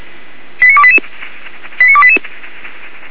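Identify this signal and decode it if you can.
Five-tone selective call (ZVEI), heard as quick runs of five tones, tones of about 70 ms. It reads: 89288 89288
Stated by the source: ZVEI five-tone selective call 89293, received by SDR, from the feuerwehr-orlamuende.org (Feuerwehr Orlamünde) server.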